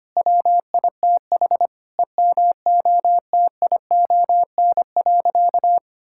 Morse code, heard as a keyed tone, about 25 wpm, 700 Hz. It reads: WITH EMOTION.
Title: Sentences from Top 1600 Words 25wpm